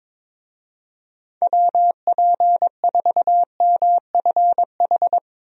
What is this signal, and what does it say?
Morse code, heard as a keyed tone, about 22 wpm, 700 Hz. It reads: WP4MFH
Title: USA Call Signs - 6 Characters - 22wpm